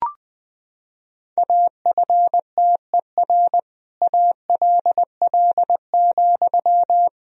Morse code, heard as a keyed tone, about 20 wpm, 700 Hz. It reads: AFTER ALL,
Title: Sentences from Top 1000 Words - Review Part 2 - 20wpm